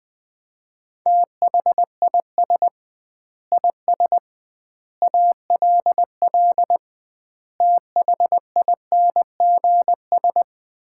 Morse code, heard as a keyed tone, about 21 wpm, 700 Hz. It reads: THIS IS ALL THINGS
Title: 2x Word-Spacing - Sets of 5 Words - 20wpm